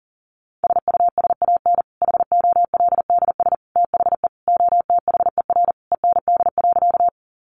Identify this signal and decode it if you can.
Morse code, heard as a keyed tone, about 40 wpm, 700 Hz. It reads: SUSAN HOLDS THE OTHER END.
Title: Easy Phrases 40wpm